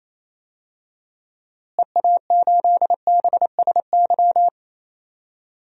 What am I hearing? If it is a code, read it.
Morse code, heard as a keyed tone, about 28 wpm, 700 Hz. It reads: EA8BSY
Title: DXCC-100 Call Signs - 6 Characters - Repeated 2x - 28wpm